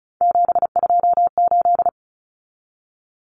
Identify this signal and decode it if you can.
Morse code, heard as a keyed tone, about 34 wpm, 700 Hz. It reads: 728
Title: Common Cut Numbers - 3-Digit 35wpm